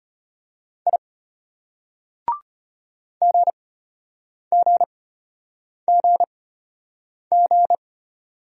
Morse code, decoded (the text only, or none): I GGGG